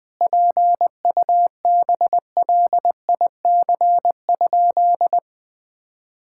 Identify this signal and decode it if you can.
Morse code, heard as a keyed tone, about 21 wpm, 700 Hz. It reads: PUBLIC?
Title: Sentences from Top 1000 Words - Review Part 5 - 20wpm